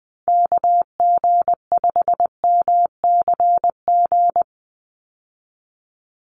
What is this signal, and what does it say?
Morse code, heard as a keyed tone, about 20 wpm, 700 Hz. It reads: KG5MCG
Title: USA Call Signs 20wpm